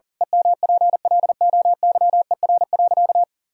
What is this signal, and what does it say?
Morse code, heard as a keyed tone, about 40 wpm, 700 Hz. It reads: EMPLOYER.
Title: Pangram Challenge v2 - 40wpm